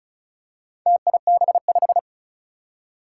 Morse code, heard as a keyed tone, about 36 wpm, 700 Hz. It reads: TIB5
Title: Four Challenging Letter-Number Combinations 35wpm